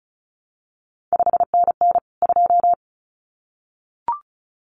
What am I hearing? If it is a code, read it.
Morse code, heard as a keyed tone, about 36 wpm, 700 Hz. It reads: 5NN 2
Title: CQ WW DX Contest Exchange - 35wpm